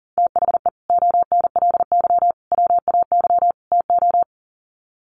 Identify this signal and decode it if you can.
Morse code, heard as a keyed tone, about 40 wpm, 700 Hz. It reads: THE ONLY WAY TO
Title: Sets of 4 Words 40wpm